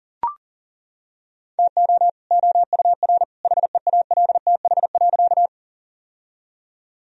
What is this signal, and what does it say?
Morse code, heard as a keyed tone, about 40 wpm, 700 Hz. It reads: TO OUR HEALTH.
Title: Sentences from Top 1000 Words - Review Part 3 - 40wpm